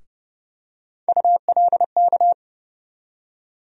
Morse code, heard as a keyed tone, about 30 wpm, 700 Hz. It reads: ULK
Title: USA Call Sign Postfixes - 3 Letters 30wpm